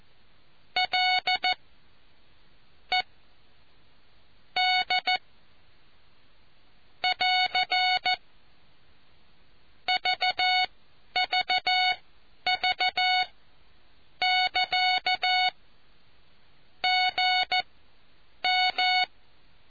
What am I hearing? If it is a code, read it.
Morse code, heard as a keyed tone, about 14 wpm, spaced as for 8 wpm, 740 Hz. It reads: L E D <AR> VVV<KA> GM